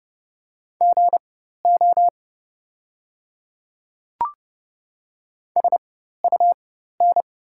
Morse code, heard as a keyed tone, about 31 wpm, spaced as for 14 wpm, 690 Hz. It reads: GO SUN
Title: 4x Character Spacing - Top 500 Words - 30wpm